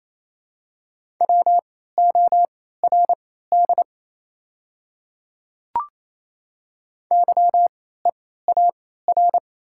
Morse code, heard as a keyed tone, about 28 wpm, spaced as for 16 wpm, 690 Hz. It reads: WORD YEAR
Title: 3x Character Spacing - Top 200 Words - 28wpm